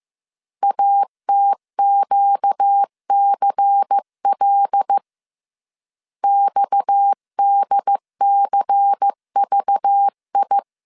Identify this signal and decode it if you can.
Morse code, heard as a keyed tone, about 15 wpm, 800 Hz. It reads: ATQCL XDCVI
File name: CW / DET / 1 6 1500C S C 14wpm 2951